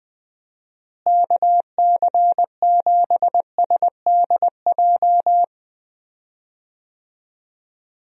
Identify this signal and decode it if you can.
Morse code, heard as a keyed tone, about 21 wpm, 700 Hz. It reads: KC7SDJ
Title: USA Call Signs - 6 Characters - Repeated 3x - 20wpm